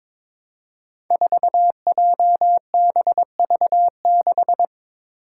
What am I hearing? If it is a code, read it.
Morse code, heard as a keyed tone, about 22 wpm, 700 Hz. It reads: 4JBV6
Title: Five Challenging Letter-Number Combinations 22wpm